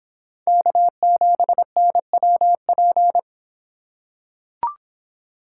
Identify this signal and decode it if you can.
Morse code, heard as a keyed tone, about 24 wpm, 700 Hz. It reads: K7NWP